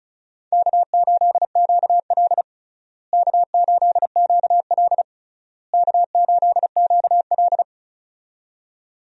Morse code, heard as a keyed tone, about 34 wpm, 700 Hz. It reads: K8QL K8QL K8QL
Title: USA Call Signs - 4 Characters - Repeated 3x - 35wpm